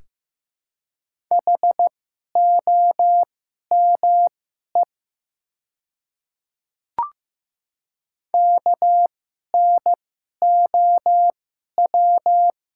Morse code, heard as a keyed tone, about 15 wpm, spaced as for 11 wpm, 700 Hz. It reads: HOME KNOW
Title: 2x Character Spacing - Top 200 Words - 15wpm